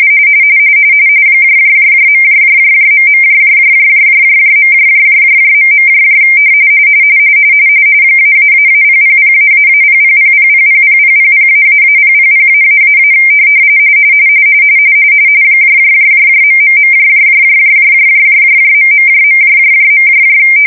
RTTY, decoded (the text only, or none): ...... ----000---0000000000---000---- .......